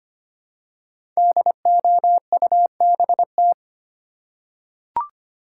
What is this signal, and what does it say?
Morse code, heard as a keyed tone, about 25 wpm, 700 Hz. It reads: DOUBT
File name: 5 Letter Words - First 3 Letters Given - 25wpm